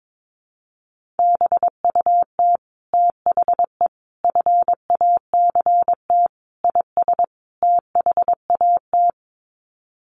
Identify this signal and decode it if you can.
Morse code, heard as a keyed tone, about 22 wpm, 700 Hz. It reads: BUT THE FACT IS THAT
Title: Sets of 5 Words Encore2 22wpm